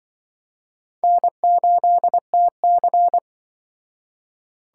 Morse code, heard as a keyed tone, about 23 wpm, 700 Hz. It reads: N8TC
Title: Speed Racing - USA 1x2 Call Signs 20wpm